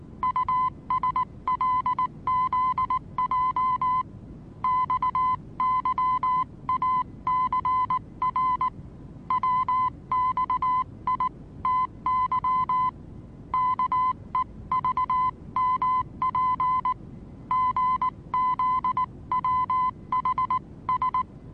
Morse code, heard as a keyed tone, about 19 wpm, 1020 Hz. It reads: USLZJ XYACR WXITY KEVMP GZWHS
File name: repos / Morsekurs / DJ4UF Morsen26